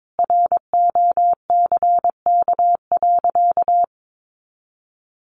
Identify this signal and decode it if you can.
Morse code, heard as a keyed tone, about 22 wpm, 700 Hz. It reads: ROCK.
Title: Sentences from Top 1500 Words 22wpm